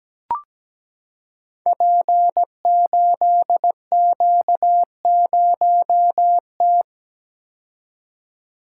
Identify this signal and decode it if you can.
Morse code, heard as a keyed tone, about 17 wpm, 700 Hz. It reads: P8Q0T
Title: Code Groups - 5 Letter:Numbers - 17wpm